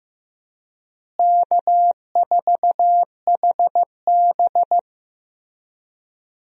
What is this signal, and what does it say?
Morse code, heard as a keyed tone, about 15 wpm, 700 Hz. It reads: K4HB
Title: USA Call Signs - 4 Characters - Repeated 3x - 15wpm